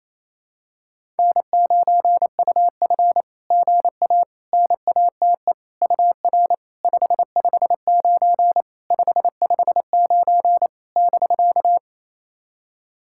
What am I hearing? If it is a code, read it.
Morse code, heard as a keyed tone, about 29 wpm, 700 Hz. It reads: N9UF GA NATE UR 559 559 <BK>
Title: SOTA QSO Practice 28wpm